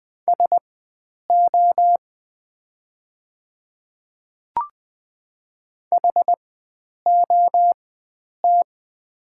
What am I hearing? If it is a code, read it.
Morse code, heard as a keyed tone, about 20 wpm, spaced as for 9 wpm, 700 Hz. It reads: SO HOT
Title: 4x Character Spacing - Top 100 Words - 20wpm